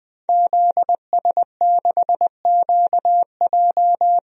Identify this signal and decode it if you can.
Morse code, heard as a keyed tone, about 20 wpm, 700 Hz. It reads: ZS6QJ
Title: DXCC-100 Call Signs - 5 Characters - Repeated 2x - 20wpm